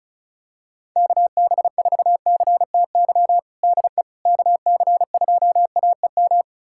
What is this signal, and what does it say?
Morse code, heard as a keyed tone, about 36 wpm, 700 Hz. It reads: KB4CTY DE KC2AEM
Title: USA Callsign DE Callsign 35wpm